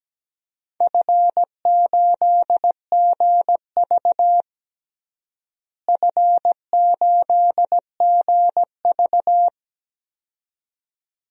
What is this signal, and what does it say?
Morse code, heard as a keyed tone, about 17 wpm, 700 Hz. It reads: F8GV F8GV
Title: DXCC-100 Call Signs - 4 Characters - Repeated 2x - 17wpm